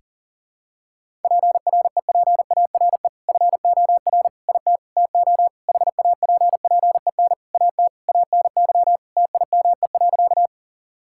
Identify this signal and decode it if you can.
Morse code, heard as a keyed tone, about 40 wpm, 700 Hz. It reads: PREPARE FOR IT TO HAPPEN AT ANY TIME.
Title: Sentences from Top 1000 Words - Review Part 6 - 40wpm